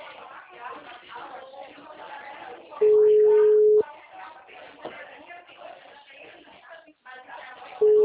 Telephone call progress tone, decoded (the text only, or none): ringback tone